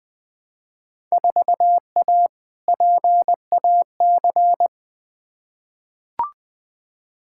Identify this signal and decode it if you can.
Morse code, heard as a keyed tone, about 20 wpm, 700 Hz. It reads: 4A PAC
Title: Field Day Exchanges Encore - 20wpm